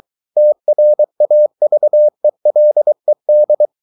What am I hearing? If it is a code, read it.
Morse code, heard as a keyed tone, about 24 wpm, 600 Hz. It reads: TRAVELED
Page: Romanen Never af Ken Follet på engelsk.